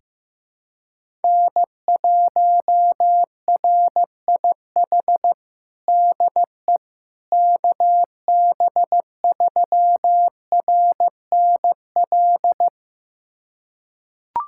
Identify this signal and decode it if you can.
Morse code, heard as a keyed tone, about 15 wpm, 700 Hz. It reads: N1RIH DE KB3RNL
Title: USA Callsign DE Callsign 15wpm